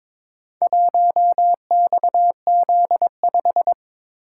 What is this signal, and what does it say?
Morse code, heard as a keyed tone, about 22 wpm, 700 Hz. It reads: 1XZ5
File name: practice / 22wpm / Four Letter-Number Combinations - 22wpm